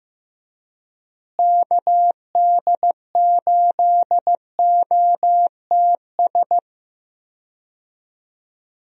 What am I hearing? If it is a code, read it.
Morse code, heard as a keyed tone, about 15 wpm, 700 Hz. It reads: KD8OTS
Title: USA Call Signs - 6 Characters - Repeated 3x - 15wpm